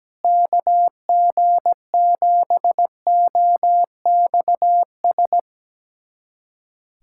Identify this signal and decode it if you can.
Morse code, heard as a keyed tone, about 17 wpm, 700 Hz. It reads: KG7OXS
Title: USA Callsign DE Callsign 17wpm